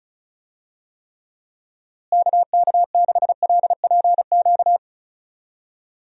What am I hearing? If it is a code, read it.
Morse code, heard as a keyed tone, about 36 wpm, 700 Hz. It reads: KK6LPQ
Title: USA Call Signs - 6 Characters - Repeated 2x - 35wpm